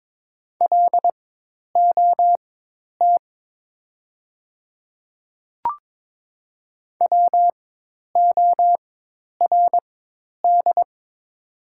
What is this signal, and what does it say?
Morse code, heard as a keyed tone, about 21 wpm, spaced as for 10 wpm, 700 Hz. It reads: LOT WORD
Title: 4x Character Spacing - Top 500 Words - 22wpm